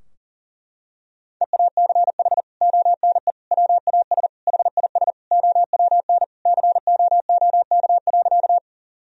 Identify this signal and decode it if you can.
Morse code, heard as a keyed tone, about 40 wpm, 700 Hz. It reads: EACH ONE WAS HIS OWN COOK.
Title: Sentences from Top 1000 Words - Review Part 4 - 40wpm